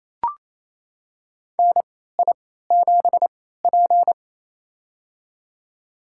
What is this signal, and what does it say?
Morse code, heard as a keyed tone, about 28 wpm, spaced as for 16 wpm, 700 Hz. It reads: NI7P
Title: USA Call Signs - 4 Characters - 3x Character Spacing - 28wpm